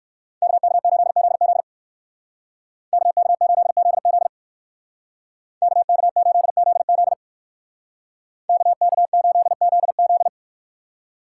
Morse code, heard as a keyed tone, about 59 wpm, 690 Hz. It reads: KK8ZZ KK8ZZ KK8ZZ KK8ZZ